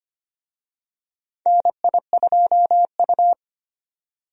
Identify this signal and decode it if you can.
Morse code, heard as a keyed tone, about 25 wpm, 700 Hz. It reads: NI2U
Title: USA Call Signs - 4 Characters - 25wpm